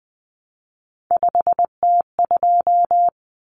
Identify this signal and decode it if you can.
Morse code, heard as a keyed tone, about 19 wpm, 700 Hz. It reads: 5T2
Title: Common Cut Numbers - 3-Digit 20wpm